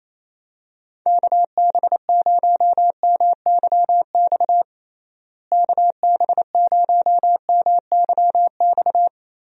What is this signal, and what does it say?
Morse code, heard as a keyed tone, about 28 wpm, 700 Hz. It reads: KB0MYX KB0MYX
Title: USA Call Signs - 6 Characters - Repeated 3x - 28wpm